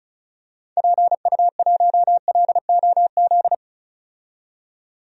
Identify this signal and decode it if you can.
Morse code, heard as a keyed tone, about 34 wpm, 700 Hz. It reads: PU1LOZ
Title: DXCC-100 Call Signs - 6 Characters - 35wpm